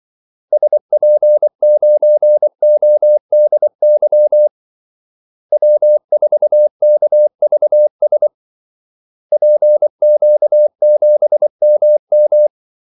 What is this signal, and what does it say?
Morse code, heard as a keyed tone, about 24 wpm, 600 Hz. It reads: SP9ODY W4KVS PQ7MM
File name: Callsigns-24wpm